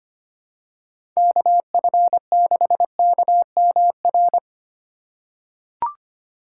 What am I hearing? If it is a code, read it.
Morse code, heard as a keyed tone, about 25 wpm, 700 Hz. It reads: KF6KMR
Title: USA Call Signs - 2x Character Spacing - 25wpm